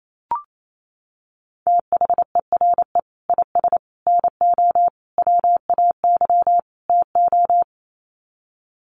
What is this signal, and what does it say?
Morse code, heard as a keyed tone, about 28 wpm, 700 Hz. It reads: THERE IS NO WAY TO
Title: Sets of 5 Words Encore 28wpm